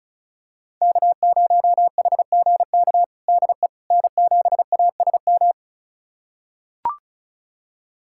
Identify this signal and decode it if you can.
Morse code, heard as a keyed tone, about 36 wpm, 700 Hz. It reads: K0HGK DE N7ASM
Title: USA Callsign DE Callsign - 5 Characters - 35wpm